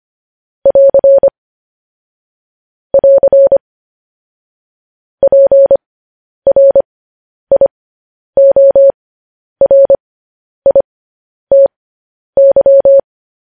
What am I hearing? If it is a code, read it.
Morse code, heard as a keyed tone, about 25 wpm, spaced as for 10 wpm, 550 Hz. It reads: <AR> <AR> PRIORITY
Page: news-headlines-in-morse-code-at-10-wpm